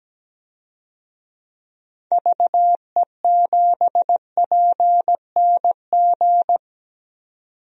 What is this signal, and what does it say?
Morse code, heard as a keyed tone, about 17 wpm, 700 Hz. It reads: VE7PNG
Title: DXCC-100 Call Signs - 6 Characters - 17wpm